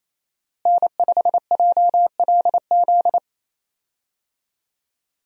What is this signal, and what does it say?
Morse code, heard as a keyed tone, about 28 wpm, 700 Hz. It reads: N5JLZ
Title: USA Call Signs - 5 Characters - Repeated 2x - 28wpm